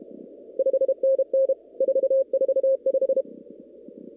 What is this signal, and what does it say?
Morse code, heard as a keyed tone, about 32 wpm, 530 Hz. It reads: 5NN 445
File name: CW / kt5j wpx cw-op-k5pi